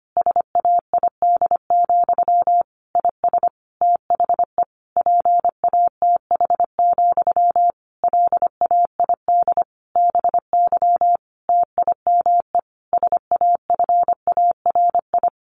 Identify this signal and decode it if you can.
Morse code, heard as a keyed tone, about 25 wpm, 700 Hz. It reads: SAID, IS THE PATH, LAID BY TIME SAFARI
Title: A Sound Of Thunder 25wpm